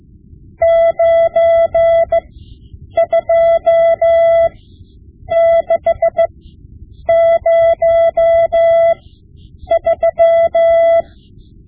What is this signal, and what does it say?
Morse code, heard as a keyed tone, about 15 wpm, spaced as for 8 wpm, 670 Hz. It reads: 92603
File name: Morse Lessons / Morse Lessons / Tape-5B